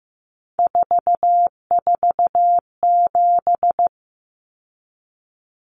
Morse code, heard as a keyed tone, about 15 wpm, 700 Hz. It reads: 447
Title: Common Cut Numbers - 3-Digit 15wpm